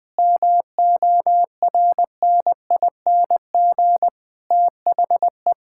Morse code, heard as a keyed tone, about 20 wpm, 700 Hz. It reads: MORNING THE
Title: Sentences with 7-Word Phrases - 20wpm